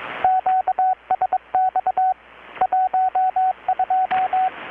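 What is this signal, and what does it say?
Morse code, heard as a keyed tone, about 22 wpm, 750 Hz, with radio noise. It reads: QSX 12
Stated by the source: Morse from KPH's Night of Nights XX maritime radiotelegraphy event, received in Chiapas, Mexico about 3,800 km away on a Kenwood R-600 with a grounded 10 m long-wire antenna.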